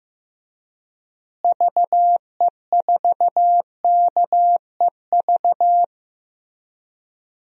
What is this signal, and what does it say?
Morse code, heard as a keyed tone, about 15 wpm, 700 Hz. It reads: VE4KEV
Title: DXCC-100 Call Signs - 6 Characters - Repeated 3x - 15wpm